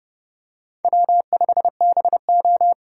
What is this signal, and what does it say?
Morse code, heard as a keyed tone, about 29 wpm, 690 Hz. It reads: W5BO